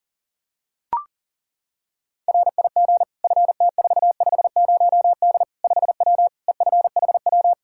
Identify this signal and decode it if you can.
Morse code, heard as a keyed tone, about 40 wpm, 700 Hz. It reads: RIG FT450D 5W EFHW
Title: Practice QSO - Exchange 2 - Info - 40wpm